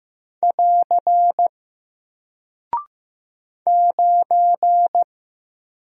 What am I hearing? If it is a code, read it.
Morse code, heard as a keyed tone, about 15 wpm, 700 Hz. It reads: <AR> 9
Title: Single Character (Letter:Number:Punctuation:Prosign) Alt 15wpm